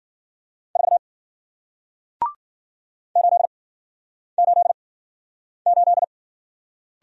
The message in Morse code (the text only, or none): 4 8 8 8